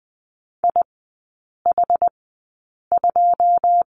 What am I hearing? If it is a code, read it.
Morse code, heard as a keyed tone, about 20 wpm, spaced as for 8 wpm, 700 Hz. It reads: IH2
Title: 127 Lesson 21e - Number 6 - Three Characters 20wpm